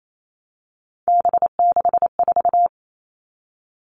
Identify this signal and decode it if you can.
Morse code, heard as a keyed tone, about 28 wpm, 700 Hz. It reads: B64
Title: All Cut Numbers - 3-Digit 28wpm